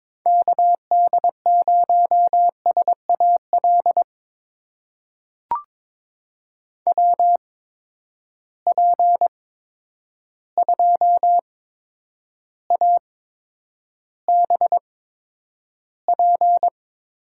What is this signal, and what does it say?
Morse code, heard as a keyed tone, about 22 wpm, spaced as for 9 wpm, 700 Hz. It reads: KD0SAL WP2ABP
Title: USA Call Signs - 6 Characters - 8x Character Spacing - 22wpm